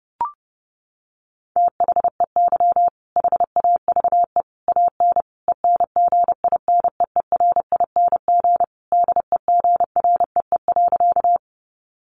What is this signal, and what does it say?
Morse code, heard as a keyed tone, about 30 wpm, 700 Hz. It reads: THEY HAVE AN ENGINEERING DEGREE.
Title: Sentences from Top 1500 Words 30wpm